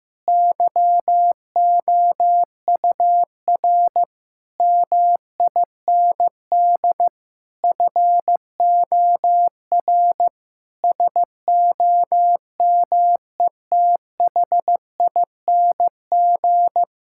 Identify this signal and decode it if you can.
Morse code, heard as a keyed tone, about 15 wpm, 700 Hz. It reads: YOUR MIND FOR SOMETHING
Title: Sentences from Top 1000 Words - Review Part 6 - 15wpm